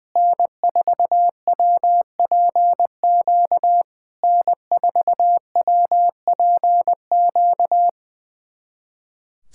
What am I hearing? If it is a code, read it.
Morse code, heard as a keyed tone, about 20 wpm, 700 Hz. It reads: N4WPQ N4WPQ
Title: SOTA QSO Practice 20wpm